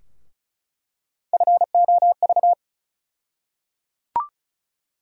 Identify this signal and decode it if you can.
Morse code, heard as a keyed tone, about 36 wpm, 700 Hz. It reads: FOV
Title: USA Call Sign Postfixes - 3 Letters 35wpm